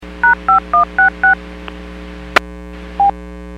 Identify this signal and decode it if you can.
DTMF, heard as telephone touch tones, tones of about 130 ms. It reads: #5166
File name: WPS / 06900 19-49 WPS 451.25 WPS